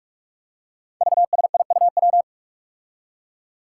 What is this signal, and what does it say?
Morse code, heard as a keyed tone, about 44 wpm, 700 Hz. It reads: VSIUW
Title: Five Challenging Letter-Number Combinations 45wpm